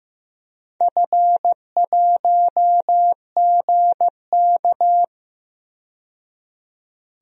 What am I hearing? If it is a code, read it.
Morse code, heard as a keyed tone, about 15 wpm, 700 Hz. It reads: F1GK
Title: DXCC-100 Call Signs - 4 Characters - 15wpm